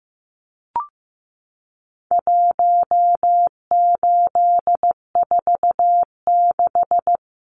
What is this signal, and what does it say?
Morse code, heard as a keyed tone, about 15 wpm, 700 Hz. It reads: 1846
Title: Numbers - Contest Style - 4-Digit 15wpm